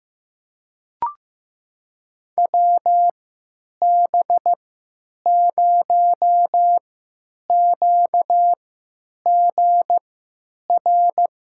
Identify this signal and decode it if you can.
Morse code, heard as a keyed tone, about 15 wpm, spaced as for 9 wpm, 700 Hz. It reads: WB0QGR
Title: USA Call Signs - 3x Character Spacing - 15wpm